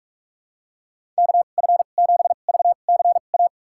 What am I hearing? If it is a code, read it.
Morse code, heard as a keyed tone, about 47 wpm, spaced as for 33 wpm, 700 Hz. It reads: KF7VCA
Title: USA Call Signs - 6 Characters - 2x Character Spacing - 45wpm